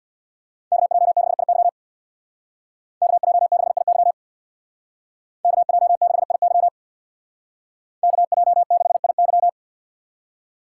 Morse code, heard as a keyed tone, about 61 wpm, 700 Hz. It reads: KJ6IY KJ6IY KJ6IY KJ6IY
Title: Speed Racing - USA 2x2 Call Signs 50wpm